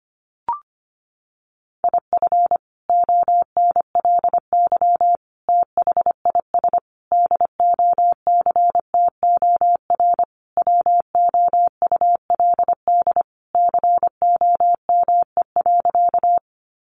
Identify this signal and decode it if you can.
Morse code, heard as a keyed tone, about 25 wpm, 700 Hz. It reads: IF ONLY THIS DOCTOR WOULD COME.
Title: Sentences from Top 1500 Words 25wpm